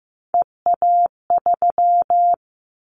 Morse code, heard as a keyed tone, about 15 wpm, 700 Hz. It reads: EA3
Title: Callsign Prefixes for Top 25 DXCC - 15wpm